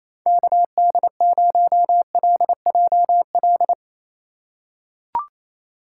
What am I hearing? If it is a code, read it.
Morse code, heard as a keyed tone, about 28 wpm, 700 Hz. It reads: KD0LJL
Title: USA Call Signs - 6 Characters - 4x Character Spacing - 28wpm